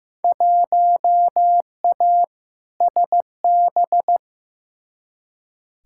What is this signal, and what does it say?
Morse code, heard as a keyed tone, about 15 wpm, 700 Hz. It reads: 1A SB
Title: Field Day Exchanges Encore - 15wpm